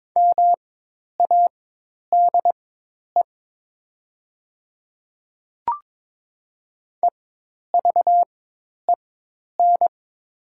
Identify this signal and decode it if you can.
Morse code, heard as a keyed tone, about 22 wpm, spaced as for 10 wpm, 700 Hz. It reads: MADE EVEN
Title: 4x Character Spacing - Top 300 Words - 22wpm